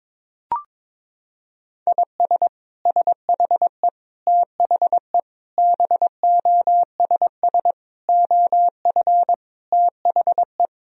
Morse code, heard as a keyed tone, about 22 wpm, 700 Hz. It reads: IS SHE THE BOSS OF THE
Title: Sentences from Top 1000 Words - Review Part 4 - 22wpm